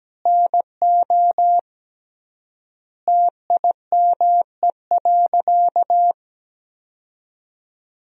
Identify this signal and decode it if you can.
Morse code, heard as a keyed tone, about 17 wpm, 700 Hz. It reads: NO TIME.
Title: 3x Word-Spacing - Sentences from Top 400 Words - 17wpm